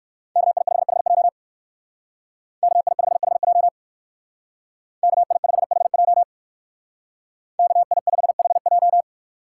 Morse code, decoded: KI5HJ KI5HJ KI5HJ KI5HJ

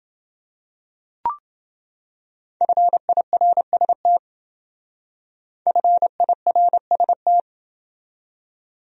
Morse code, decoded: FIRST FIRST

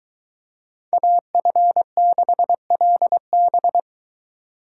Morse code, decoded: AF6LB

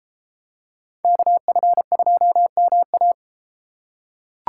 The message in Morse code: KF2MA

